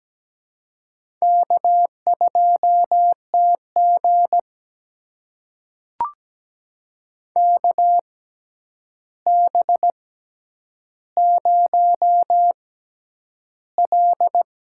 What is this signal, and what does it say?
Morse code, heard as a keyed tone, about 17 wpm, spaced as for 9 wpm, 700 Hz. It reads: K2TG KB0L
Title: USA Call Signs - 4 Characters - 6x Character Spacing - 17wpm